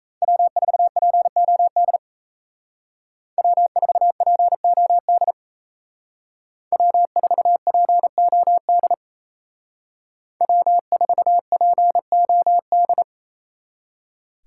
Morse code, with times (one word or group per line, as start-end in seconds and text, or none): 0.2-2.0 W4POD
3.4-5.3 W4POD
6.7-8.9 W4POD
10.4-13.0 W4POD